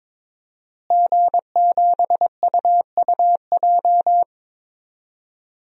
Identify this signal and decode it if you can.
Morse code, heard as a keyed tone, about 22 wpm, 700 Hz. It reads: G7UUJ
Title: DXCC-100 Call Signs - 5 Characters - Repeated 2x - 22wpm